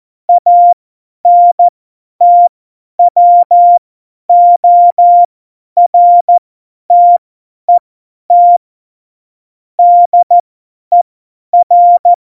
Morse code, heard as a keyed tone, about 14 wpm, spaced as for 10 wpm, 700 Hz. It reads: ANTWORTET DER
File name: Morsekurs / Witze / Witz 16. Oktober 2018, 50 F70